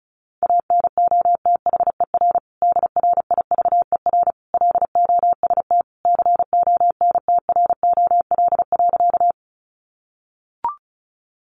Morse code, heard as a keyed tone, about 34 wpm, 700 Hz. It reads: ANOTHER DRIVER LOST CONTROL.
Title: Sentences from Top 1400 Words 35wpm